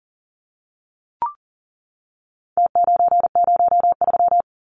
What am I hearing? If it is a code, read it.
Morse code, heard as a keyed tone, about 38 wpm, 700 Hz. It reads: T903
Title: Numbers - Contest Style - 4-Digit 40wpm